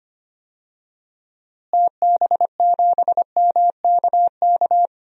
Morse code, heard as a keyed tone, about 25 wpm, 700 Hz. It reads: TB7MKK
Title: DXCC-100 Call Signs - 6 Characters - Repeated 2x - 25wpm